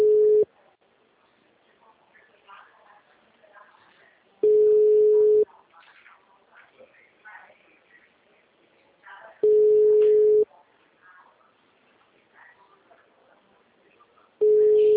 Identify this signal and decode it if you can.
Ringback tone (Europe, 425 Hz).
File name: out- -2002- - - .5657